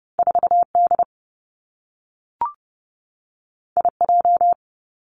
Morse code, heard as a keyed tone, about 30 wpm, 700 Hz. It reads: 4 D I J
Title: Two Challenging Letter-Number Combinations 30wpm